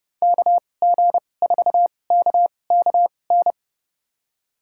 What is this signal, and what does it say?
Morse code, heard as a keyed tone, about 30 wpm, spaced as for 22 wpm, 700 Hz. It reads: KG4KKN
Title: USA Call Signs - 6 Characters - 2x Character Spacing - 30wpm